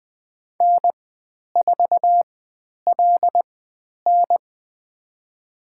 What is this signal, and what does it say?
Morse code, heard as a keyed tone, about 20 wpm, spaced as for 10 wpm, 700 Hz. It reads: N4LN
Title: 102 Lesson 17g - Letter u - Call Signs 20wpm